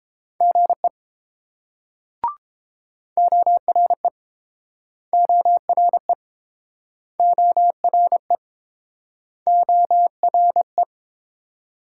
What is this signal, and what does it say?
Morse code, heard as a keyed tone, about 30 wpm, 700 Hz. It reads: GE ORE ORE ORE ORE